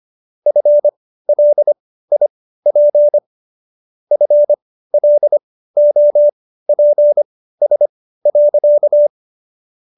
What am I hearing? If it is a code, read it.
Morse code, heard as a keyed tone, about 25 wpm, spaced as for 15 wpm, 590 Hz. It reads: FLIP FLOPS.